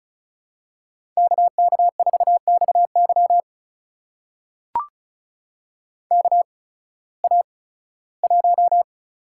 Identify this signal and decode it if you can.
Morse code, heard as a keyed tone, about 36 wpm, spaced as for 19 wpm, 700 Hz. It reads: KK4XY KA1
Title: USA Call Signs - 8x Character Spacing - 35wpm